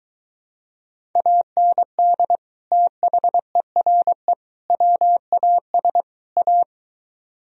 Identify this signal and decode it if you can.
Morse code, heard as a keyed tone, about 23 wpm, 700 Hz. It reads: AND THERE WAS A